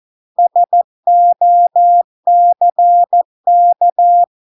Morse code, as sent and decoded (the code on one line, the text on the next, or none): ... --- -.-. -.-
SOCK